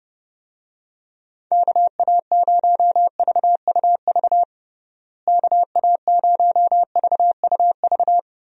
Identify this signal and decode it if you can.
Morse code, heard as a keyed tone, about 32 wpm, 700 Hz. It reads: KA0VUV KA0VUV
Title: USA Call Signs - 6 Characters - Repeated 2x - 30wpm